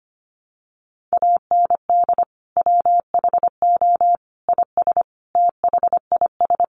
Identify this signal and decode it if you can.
Morse code, heard as a keyed tone, about 25 wpm, 700 Hz. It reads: AND WHO IS THIS
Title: Sentences from Top 300 Words 25wpm